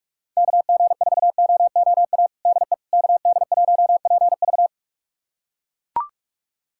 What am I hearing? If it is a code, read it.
Morse code, heard as a keyed tone, about 41 wpm, 700 Hz. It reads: KG4OOA DE KD1PV
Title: USA Callsign DE Callsign 45wpm